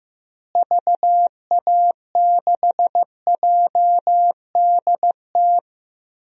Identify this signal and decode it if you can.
Morse code, heard as a keyed tone, about 15 wpm, 700 Hz. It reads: VA6JDT
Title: DXCC-100 Call Signs - 6 Characters - Repeated 3x - 15wpm